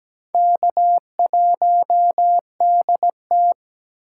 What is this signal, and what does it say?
Morse code, heard as a keyed tone, about 20 wpm, 700 Hz. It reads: K1DT
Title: Speed Racing - USA 1x2 Call Signs 17wpm